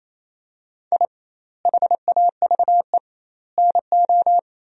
Morse code, heard as a keyed tone, about 28 wpm, 700 Hz. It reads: I HAVE NO
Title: 2x Word-Spacing - Sentences from Top 400 Words - 28wpm